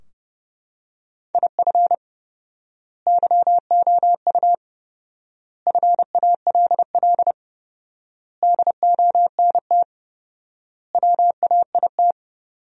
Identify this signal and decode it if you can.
Morse code, heard as a keyed tone, about 32 wpm, 700 Hz. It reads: IF YOU FALL DONT WAIT
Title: 4x Word-Spacing - Sentences from Top 500 Words - 30wpm